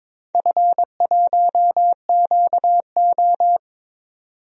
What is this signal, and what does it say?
Morse code, heard as a keyed tone, about 22 wpm, 700 Hz. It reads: F1QO
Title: DXCC-100 Call Signs - 4 Characters - 22wpm